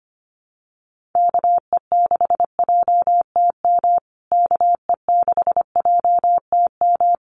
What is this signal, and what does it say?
Morse code, heard as a keyed tone, about 25 wpm, 700 Hz. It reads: KE6JTM KE6JTM
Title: USA Call Signs Repeated 3x 25wpm